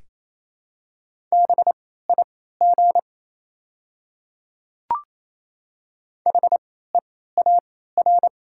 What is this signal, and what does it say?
Morse code, heard as a keyed tone, about 28 wpm, spaced as for 16 wpm, 700 Hz. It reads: BIG HEAR
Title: 3x Character Spacing - Top 300 Words - 28wpm